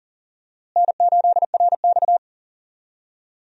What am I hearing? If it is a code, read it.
Morse code, heard as a keyed tone, about 40 wpm, 700 Hz. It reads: N8RX
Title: USA Call Signs - 4 Characters - Repeated 3x - 40wpm